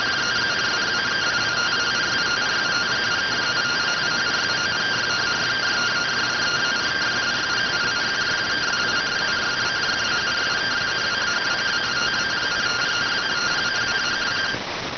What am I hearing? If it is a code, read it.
Two-tone teleprinter signal, about 170 Hz shift.